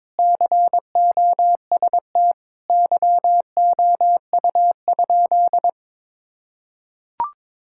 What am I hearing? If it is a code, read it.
Morse code, heard as a keyed tone, about 21 wpm, 700 Hz. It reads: COST YOU?
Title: Sentences from Top 1000 Words - Review Part 5 - 22wpm